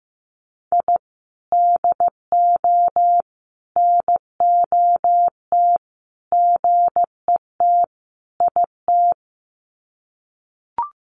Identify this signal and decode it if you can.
Morse code, heard as a keyed tone, about 15 wpm, 700 Hz. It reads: I DO NOT GET IT
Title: Sets of 5 Words Encore 15wpm